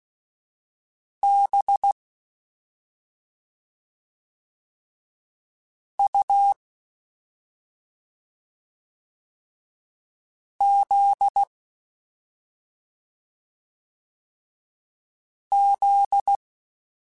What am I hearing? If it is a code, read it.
Morse code, heard as a keyed tone, about 16 wpm, 800 Hz. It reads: B U Z Z
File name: Special INGs-mp3s / Special INGs-mp3s / Slow ing @10